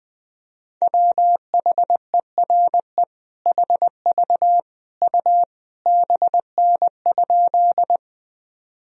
Morse code, heard as a keyed tone, about 20 wpm, 700 Hz. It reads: WHERE HV U BN?
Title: Sentences from Top 200 Words and Abbreviations 20wpm